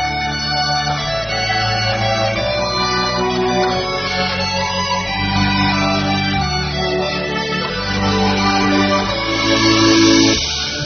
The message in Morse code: none